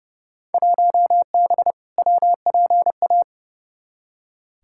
Morse code, heard as a keyed tone, about 32 wpm, 700 Hz. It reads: 1B WPA
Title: Field Day Exchanges Encore - 30wpm